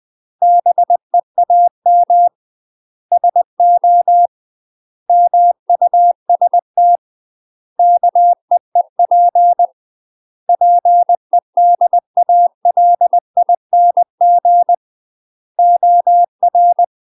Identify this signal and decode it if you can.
Morse code, heard as a keyed tone, about 20 wpm, 700 Hz. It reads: BEAM SO MUST KEEP PEDALING OR
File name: UK MASTER QSOs TRAINING / 20WPM / UK MASTER QSOs 0028